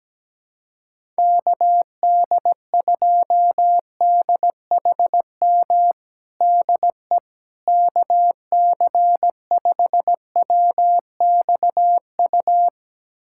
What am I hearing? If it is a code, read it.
Morse code, heard as a keyed tone, about 17 wpm, 700 Hz. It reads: KD2DHM DE KC5WXU
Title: USA Callsign DE Callsign - 6 Characters - 17wpm